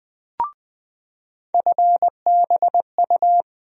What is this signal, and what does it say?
Morse code, heard as a keyed tone, about 20 wpm, 700 Hz. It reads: FBU
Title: USA Call Sign Postfixes - 3 Letters 20wpm